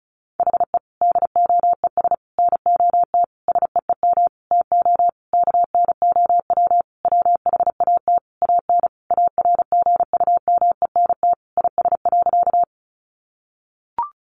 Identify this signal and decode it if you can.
Morse code, heard as a keyed tone, about 34 wpm, 700 Hz. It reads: HE DOES NOT SEEM TO KNOW WHAT AN ARGUMENT IS.
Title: Sentences from Top 1400 Words 35wpm